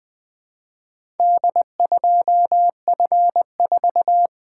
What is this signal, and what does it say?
Morse code, heard as a keyed tone, about 20 wpm, 700 Hz. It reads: D2F4
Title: Four Letter-Number Combinations - 20wpm